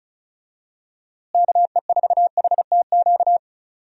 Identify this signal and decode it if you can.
Morse code, heard as a keyed tone, about 34 wpm, 700 Hz. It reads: KE4HTQ
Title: USA Call Signs - 6 Characters - Repeated 2x - 35wpm